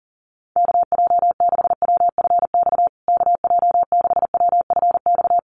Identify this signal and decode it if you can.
Morse code, heard as a keyed tone, about 40 wpm, 700 Hz. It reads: KJ6WFX KJ6WFX
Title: USA Call Signs Repeated 3x 40wpm